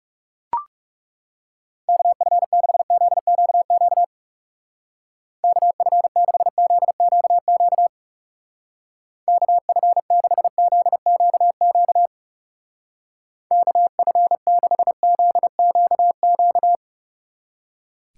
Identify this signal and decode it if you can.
Morse code, heard as a keyed tone, about 37 wpm, 700 Hz. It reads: KF6ZQQ KF6ZQQ KF6ZQQ KF6ZQQ
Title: Speed Racing - USA 2x3 Call Signs 30wpm